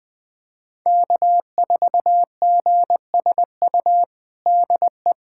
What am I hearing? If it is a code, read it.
Morse code, heard as a keyed tone, about 21 wpm, 700 Hz. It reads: K4GSU DE
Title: USA Callsign DE Callsign 20wpm